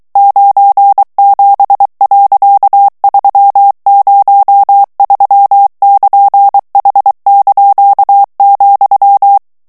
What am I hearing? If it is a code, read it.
Morse code, heard as a keyed tone, about 23 wpm, 800 Hz. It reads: 97.303<KN>H),
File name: ANGLAIS texte clair 3 vitesse 22 mots minute